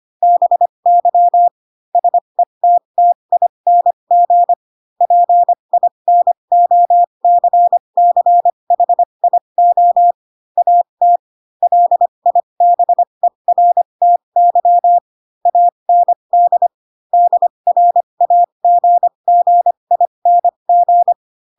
Morse code, heard as a keyed tone, about 25 wpm, spaced as for 22 wpm, 700 Hz. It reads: BY SETTING PINOCCHIO AT LIBERTY AND DRAGGING
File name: Pinocchio / Pinocchio ch3 18 25 ik1hge